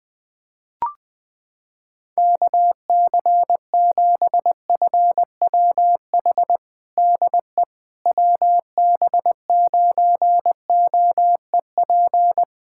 Morse code, heard as a keyed tone, about 20 wpm, 700 Hz. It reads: KC7FWH DE WB9OEP
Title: USA Callsign DE Callsign - 6 Characters - 20wpm